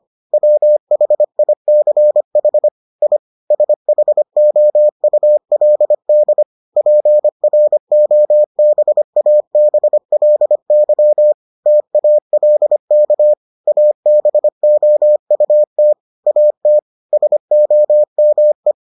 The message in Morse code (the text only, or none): WHICH I SHOULD PROBABLY TALK ABOUT AT SOME